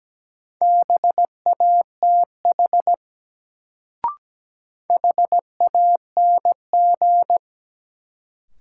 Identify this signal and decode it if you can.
Morse code, heard as a keyed tone, about 17 wpm, 700 Hz. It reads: BATH HANG
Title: Sentences from Top 1600 Words - New Word Drill - 17wpm